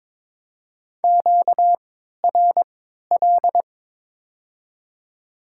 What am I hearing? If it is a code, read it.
Morse code, heard as a keyed tone, about 22 wpm, 700 Hz. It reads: Q R L
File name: Top 20 QSO Elements - 3x Character Spacing - 22wpm